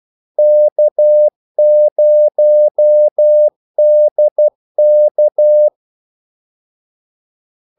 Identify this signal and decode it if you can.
Morse code, heard as a keyed tone, about 12 wpm, 600 Hz. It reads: K0DK
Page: 2 ekstra ordmellemrum mellem kaldesignalerne